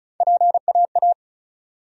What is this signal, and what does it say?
Morse code, heard as a keyed tone, about 36 wpm, 700 Hz. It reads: PAA
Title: USA Call Sign Postfixes - 3 Letters 35wpm